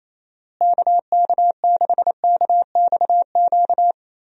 Morse code, heard as a keyed tone, about 27 wpm, 700 Hz. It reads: KK6KXQ
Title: USA Call Signs - 6 Characters - Repeated 2x - 28wpm